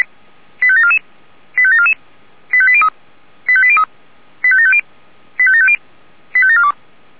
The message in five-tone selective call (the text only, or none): 85744 85744 85803 85803 85750 85750 85742